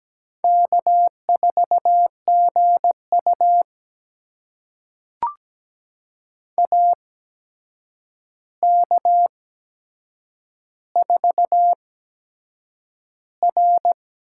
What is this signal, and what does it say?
Morse code, heard as a keyed tone, about 17 wpm, spaced as for 7 wpm, 700 Hz. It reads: K4GU AK4R
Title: USA Call Signs - 8x Character Spacing - 17wpm